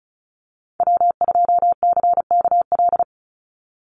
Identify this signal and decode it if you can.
Morse code, heard as a keyed tone, about 34 wpm, 700 Hz. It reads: W2CKL
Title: NA SOTA Activator Callsigns 35wpm